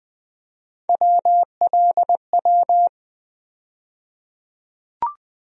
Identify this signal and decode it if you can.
Morse code, heard as a keyed tone, about 20 wpm, 700 Hz. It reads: WLW
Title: USA Call Sign Postfixes - 3 Letters 20wpm